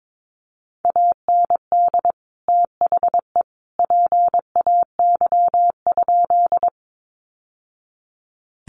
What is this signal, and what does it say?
Morse code, heard as a keyed tone, about 22 wpm, 700 Hz. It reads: AND THE PAY?
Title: Sentences from Top 700 Words 22wpm